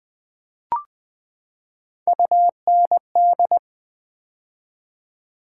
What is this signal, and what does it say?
Morse code, heard as a keyed tone, about 20 wpm, 700 Hz. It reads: UND
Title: Build A Word - 3 Letters At A Time - 9-Letter Words - 20wpm